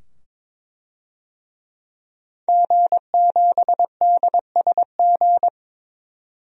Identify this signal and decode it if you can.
Morse code, heard as a keyed tone, about 22 wpm, 700 Hz. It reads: G7DSG
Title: DXCC-100 Call Signs - 5 Characters - 22wpm